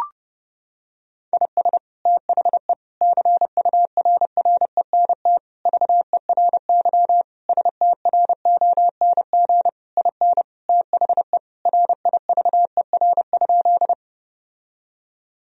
Morse code, heard as a keyed tone, about 32 wpm, 700 Hz. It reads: IS THE CURRENT VERY STRONG IN THE RIVER?
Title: Sentences from Top 1000 Words - Review Part 4 - 30wpm